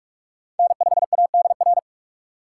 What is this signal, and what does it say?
Morse code, heard as a keyed tone, about 46 wpm, 690 Hz. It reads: N5ADR
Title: USA Call Signs - 5 Characters - 6x Character Spacing - 45wpm